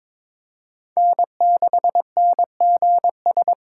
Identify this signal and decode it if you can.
Morse code, heard as a keyed tone, about 22 wpm, 700 Hz. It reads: N6NGS
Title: USA Call Signs - 5 Characters - 3x Character Spacing - 22wpm